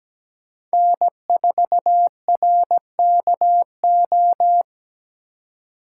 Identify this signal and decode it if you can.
Morse code, heard as a keyed tone, about 17 wpm, 700 Hz. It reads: N4RKO
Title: USA Call Signs - 5 Characters - 2x Character Spacing - 17wpm